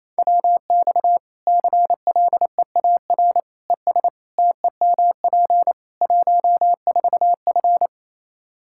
Morse code, heard as a keyed tone, about 28 wpm, 700 Hz. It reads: WX CLEAR ES TEMP 14F
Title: K1USN SST QSO Practice 28wpm